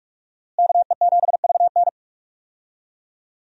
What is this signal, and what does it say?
Morse code, heard as a keyed tone, about 43 wpm, 700 Hz. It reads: KE7VN
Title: USA Call Signs - 5 Characters - Repeated 2x - 45wpm